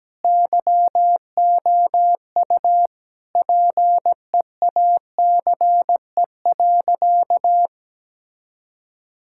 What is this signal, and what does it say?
Morse code, heard as a keyed tone, about 17 wpm, 700 Hz. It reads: YOU PEACE.
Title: Sentences from Top 1600 Words 17wpm